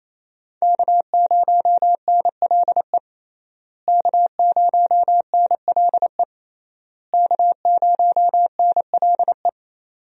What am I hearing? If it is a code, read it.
Morse code, heard as a keyed tone, about 28 wpm, 700 Hz. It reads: K0NLE K0NLE K0NLE
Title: USA Call Signs - 5 Characters - Repeated 3x - 28wpm